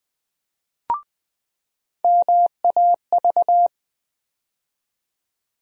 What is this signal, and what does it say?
Morse code, heard as a keyed tone, about 21 wpm, 700 Hz. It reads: MAV
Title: USA Call Sign Postfixes - 3 Letters 20wpm